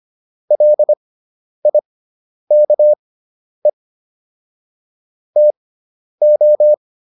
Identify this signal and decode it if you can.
Morse code, heard as a keyed tone, about 25 wpm, spaced as for 10 wpm, 600 Hz. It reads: LIKE TO